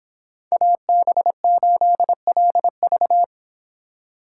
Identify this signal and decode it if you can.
Morse code, heard as a keyed tone, about 26 wpm, 700 Hz. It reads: AB8LV